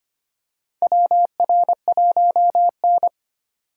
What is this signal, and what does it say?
Morse code, heard as a keyed tone, about 25 wpm, 700 Hz. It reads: WR1N
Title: USA Call Signs - 4 Characters - 25wpm